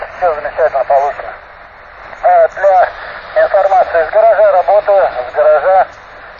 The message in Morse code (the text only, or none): none